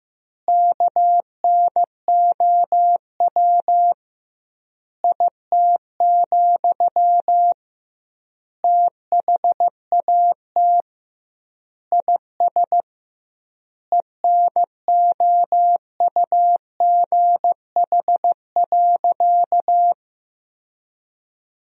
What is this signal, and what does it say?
Morse code, heard as a keyed tone, about 15 wpm, 700 Hz. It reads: KNOW IT, THAT IS ENOUGH.
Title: 2x Word-Spacing - Sentences from Top 400 Words - 15wpm